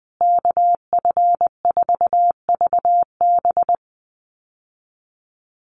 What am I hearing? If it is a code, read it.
Morse code, heard as a keyed tone, about 19 wpm, 700 Hz. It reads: KF4VB
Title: USA Call Signs Repeated 2x 20wpm